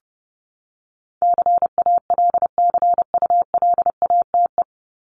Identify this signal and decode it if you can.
Morse code, heard as a keyed tone, about 30 wpm, 700 Hz. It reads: CALCULATE
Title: Top 1751-2000 Words 30wpm